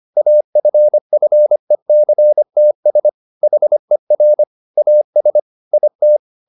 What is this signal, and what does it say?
Morse code, heard as a keyed tone, about 25 wpm, 600 Hz. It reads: AFFECTS HER AS IT
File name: the old man and the sea part 1 25wpm